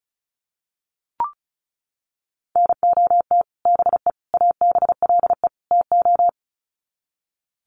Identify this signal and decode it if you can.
Morse code, heard as a keyed tone, about 34 wpm, 700 Hz. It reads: NOT BE ABLE TO
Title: Sets of 4 Words 35wpm